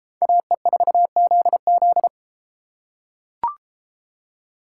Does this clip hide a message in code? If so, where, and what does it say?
Yes: Morse code from 0.2 to 2.1 s, heard as a keyed tone, about 32 wpm, 700 Hz. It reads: AE4ZZ